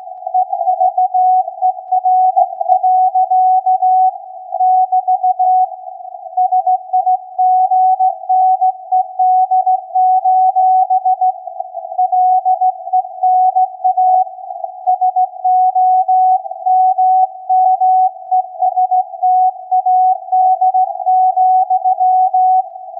Morse code, heard as a keyed tone, about 16 wpm, 740 Hz. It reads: EVER. <BT> SIGNED: LENA SOMMESTAD,
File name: iz7slz / VLF / SAQ 16-11-2022